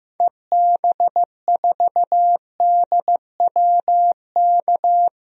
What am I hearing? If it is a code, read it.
Morse code, heard as a keyed tone, about 15 wpm, 700 Hz. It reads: EB4DWK
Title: DXCC-100 Call Signs - 6 Characters - Repeated 3x - 15wpm